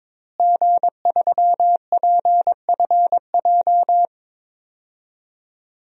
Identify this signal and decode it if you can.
Morse code, heard as a keyed tone, about 22 wpm, 700 Hz. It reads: G3PFJ
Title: DXCC-100 Call Signs - 5 Characters - Repeated 2x - 22wpm